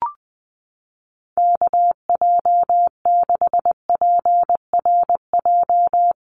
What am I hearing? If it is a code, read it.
Morse code, heard as a keyed tone, about 20 wpm, 700 Hz. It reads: KJ6PRJ
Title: USA Call Signs 20wpm